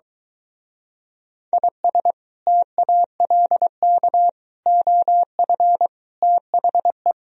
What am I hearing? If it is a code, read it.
Morse code, heard as a keyed tone, about 22 wpm, 700 Hz. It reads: IS TALK OF THE